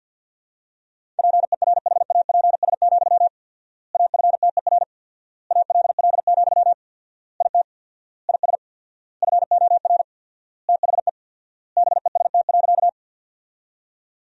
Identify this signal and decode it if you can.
Morse code, heard as a keyed tone, about 48 wpm, 700 Hz. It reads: PERHAPS, AFTER ALL, IT IS FOR THE BEST.
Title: 4x Word-Spacing - Sentences from Top 500 Words - 50wpm